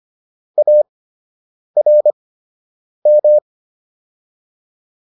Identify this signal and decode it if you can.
Morse code, heard as a keyed tone, about 25 wpm, spaced as for 8 wpm, 600 Hz. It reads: ARM